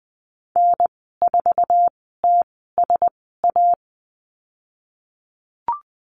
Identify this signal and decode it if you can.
Morse code, heard as a keyed tone, about 20 wpm, spaced as for 14 wpm, 700 Hz. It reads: N4TSA
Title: 069 Lesson 12g - Letter d - Call Signs 20wpm